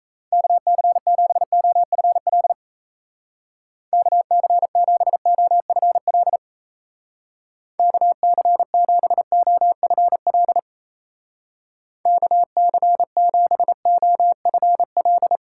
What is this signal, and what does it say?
Morse code, heard as a keyed tone, about 35 wpm, 700 Hz. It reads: KC7OFL KC7OFL KC7OFL KC7OFL
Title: Speed Racing - USA 2x3 Call Signs 28wpm